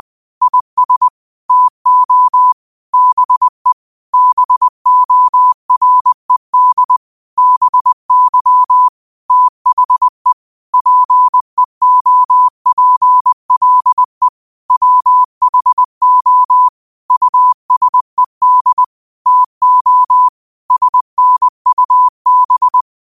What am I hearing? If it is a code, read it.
Morse code, heard as a keyed tone, about 20 wpm, 1000 Hz. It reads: IS TO BE BORED BY THE PEOPLE WHO USED TO SNUB
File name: qotdfiles / 20wpm